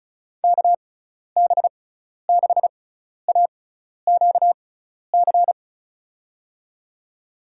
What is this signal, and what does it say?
Morse code, heard as a keyed tone, about 36 wpm, spaced as for 12 wpm, 700 Hz. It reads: KB6AQC
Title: USA Call Signs - 6 Characters - 6x Character Spacing - 35wpm